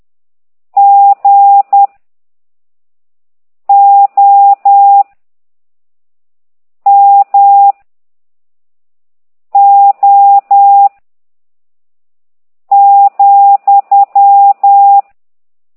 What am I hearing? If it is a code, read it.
Morse code, heard as a keyed tone, about 10 wpm, 800 Hz. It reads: G O M O ,
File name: ct2idw / exerc15